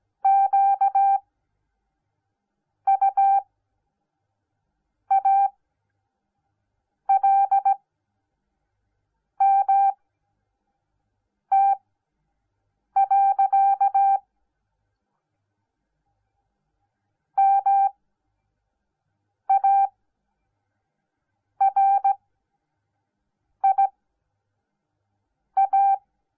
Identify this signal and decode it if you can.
Morse code, heard as a keyed tone, about 17 wpm, spaced as for 5 wpm, 790 Hz. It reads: QUALMT. MARIA